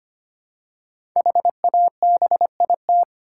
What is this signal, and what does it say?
Morse code, heard as a keyed tone, about 25 wpm, 700 Hz. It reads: HABIT
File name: 5 Letter Words - First 3 Letters Given - 25wpm